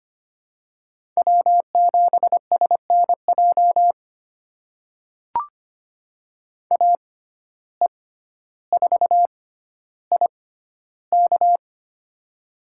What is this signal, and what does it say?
Morse code, heard as a keyed tone, about 25 wpm, spaced as for 13 wpm, 700 Hz. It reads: W7SNJ AE4IK
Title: USA Call Signs - 5 Characters - 6x Character Spacing - 25wpm